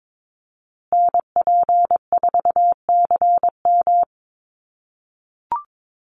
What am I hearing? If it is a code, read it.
Morse code, heard as a keyed tone, about 22 wpm, 700 Hz. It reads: NP4CM
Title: USA Call Signs Repeated 3x 22wpm